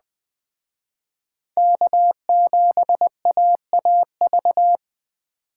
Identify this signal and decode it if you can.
Morse code, heard as a keyed tone, about 20 wpm, 700 Hz. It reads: K7AAV
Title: Speed Racing - USA 1x3 Call Signs 17wpm